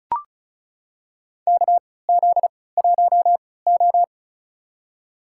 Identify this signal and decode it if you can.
Morse code, heard as a keyed tone, about 36 wpm, spaced as for 20 wpm, 700 Hz. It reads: KZ1O
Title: USA Call Signs - 4 Characters - 3x Character Spacing - 35wpm